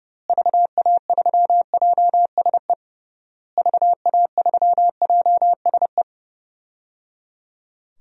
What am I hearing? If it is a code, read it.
Morse code, heard as a keyed tone, about 30 wpm, 700 Hz. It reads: VA3JSE VA3JSE
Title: DXCC-100 Call Signs - 6 Characters - Repeated 3x - 30wpm